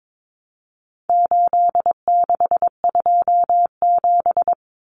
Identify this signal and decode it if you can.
Morse code, heard as a keyed tone, about 22 wpm, 700 Hz. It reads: 8627
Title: Numbers - 4-Digit 22wpm